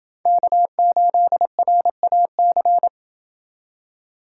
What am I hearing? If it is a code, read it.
Morse code, heard as a keyed tone, about 28 wpm, 700 Hz. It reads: K8RAC